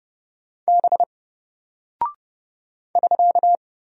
Morse code, heard as a keyed tone, about 30 wpm, 700 Hz. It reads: B <SK>
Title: Single Character (Letter:Number:Punctuation:Prosign) Alt 30wpm